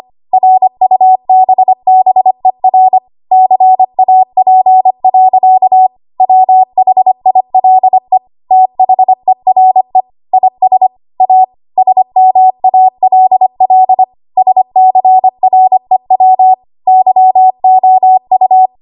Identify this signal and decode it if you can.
Morse code, heard as a keyed tone, about 25 wpm, 750 Hz. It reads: RUBBER CAP. WHILE THERE IS A SMALL SCREW YOU
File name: Archive / 25 WPM / 25WPM